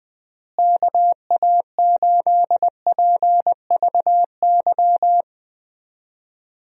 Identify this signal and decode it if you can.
Morse code, heard as a keyed tone, about 20 wpm, 700 Hz. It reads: KA8PVY
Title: USA Call Signs - 6 Characters - Repeated 3x - 20wpm